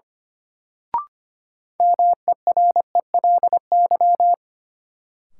Morse code, heard as a keyed tone, about 25 wpm, 700 Hz. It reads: MERELY
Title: Sentences from Top 1400 Words - New Word Drill Alt - 25wpm